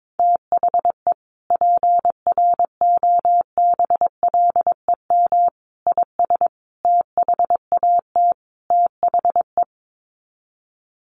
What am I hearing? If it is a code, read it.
Morse code, heard as a keyed tone, about 22 wpm, 700 Hz. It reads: THE PROBLEM IS THAT THE
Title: Sets of 5 Words Encore2 22wpm